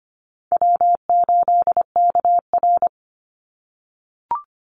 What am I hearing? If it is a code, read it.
Morse code, heard as a keyed tone, about 25 wpm, 700 Hz. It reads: W8KR
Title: Easy Call Signs - 25wpm